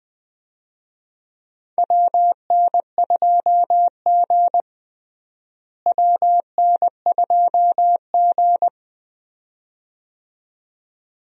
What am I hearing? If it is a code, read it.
Morse code, heard as a keyed tone, about 20 wpm, 700 Hz. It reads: WN2G WN2G
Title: USA Call Signs - 4 Characters - Repeated 2x - 20wpm